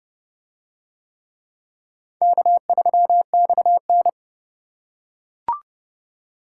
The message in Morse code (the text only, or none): K3XN